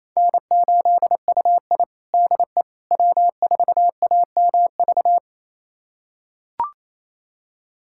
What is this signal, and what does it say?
Morse code, heard as a keyed tone, about 28 wpm, 700 Hz. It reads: N8UI DE W4AMV
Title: USA Callsign DE Callsign 28wpm